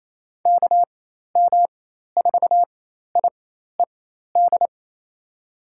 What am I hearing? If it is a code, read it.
Morse code, heard as a keyed tone, about 27 wpm, spaced as for 13 wpm, 700 Hz. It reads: KM4IED
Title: USA Call Signs - 4x Character Spacing - 28wpm